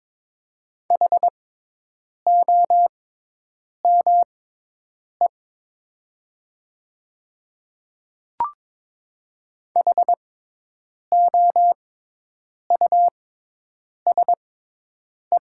Morse code, heard as a keyed tone, about 22 wpm, spaced as for 8 wpm, 700 Hz. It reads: HOME HOUSE
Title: 6x Character Spacing - Top 500 Words - 22wpm